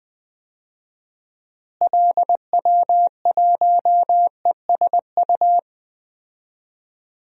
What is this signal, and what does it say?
Morse code, heard as a keyed tone, about 20 wpm, 700 Hz. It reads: LW1ESU
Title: DXCC-100 Call Signs - 6 Characters - Repeated 3x - 20wpm